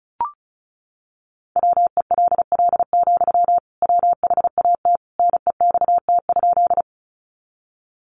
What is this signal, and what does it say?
Morse code, heard as a keyed tone, about 34 wpm, 700 Hz. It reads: WELL, WHAT NEXT?
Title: Sentences from Top 300 Words 35wpm